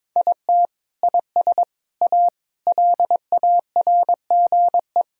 Morse code, heard as a keyed tone, about 22 wpm, 700 Hz. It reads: IT IS A LARGE
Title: Sentences from Top 1000 Words - Review Part 4 - 22wpm